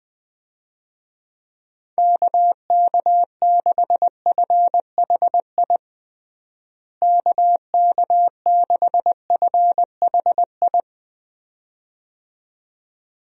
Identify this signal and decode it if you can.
Morse code, heard as a keyed tone, about 20 wpm, 700 Hz. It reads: KK6FHI KK6FHI
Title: USA Call Signs - 6 Characters - Repeated 2x - 20wpm